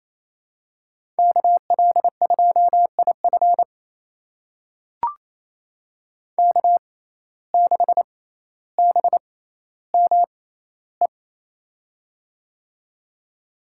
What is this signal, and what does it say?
Morse code, heard as a keyed tone, about 28 wpm, spaced as for 14 wpm, 700 Hz. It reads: KL2IF K6BME
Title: USA Call Signs - 5 Characters - 6x Character Spacing - 28wpm